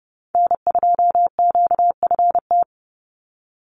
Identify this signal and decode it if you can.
Morse code, heard as a keyed tone, about 30 wpm, 700 Hz. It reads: N2QFT
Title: USA Call Signs 30wpm